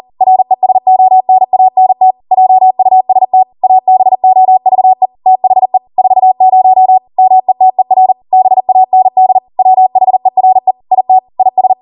Morse code, heard as a keyed tone, about 40 wpm, 750 Hz. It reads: RESONANT JUST ABOVE THE 40 METER BAND WHERE IT IS